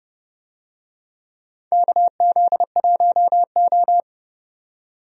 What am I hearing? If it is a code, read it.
Morse code, heard as a keyed tone, about 30 wpm, 700 Hz. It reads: KZ1O
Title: USA Call Signs - 4 Characters - 30wpm